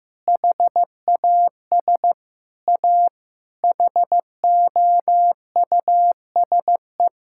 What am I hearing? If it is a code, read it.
Morse code, heard as a keyed tone, about 15 wpm, 700 Hz. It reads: HAS A HOUSE
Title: Sentences from Top 1000 Words - Review Part 5 - 15wpm